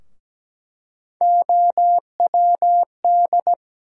Morse code, heard as a keyed tone, about 17 wpm, 700 Hz. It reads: OWD
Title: USA Call Sign Postfixes - 3 Letters 17wpm